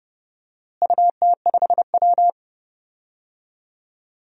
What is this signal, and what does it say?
Morse code, heard as a keyed tone, about 31 wpm, 700 Hz. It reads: UT5W
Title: DXCC-100 Call Signs - 4 Characters - Repeated 2x - 30wpm